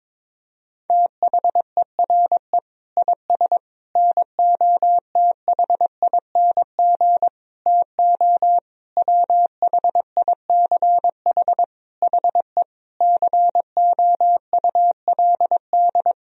Morse code, heard as a keyed tone, about 22 wpm, 700 Hz. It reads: THERE IS NOTHING TO WHICH HE COULD
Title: Sentences from Top 1000 Words - Review Part 3 - 22wpm